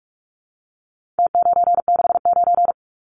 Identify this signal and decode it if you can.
Morse code, heard as a keyed tone, about 46 wpm, 700 Hz. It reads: T969
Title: Numbers - Contest Style - 4-Digit 45wpm